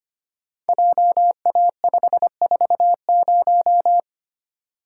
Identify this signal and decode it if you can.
Morse code, heard as a keyed tone, about 25 wpm, 700 Hz. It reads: JA540
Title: Code Groups - 5 Letter:Numbers - 25wpm